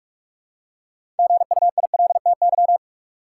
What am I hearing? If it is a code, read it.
Morse code, heard as a keyed tone, about 45 wpm, 700 Hz. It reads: GUILTY
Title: Six Letter Words - 45wpm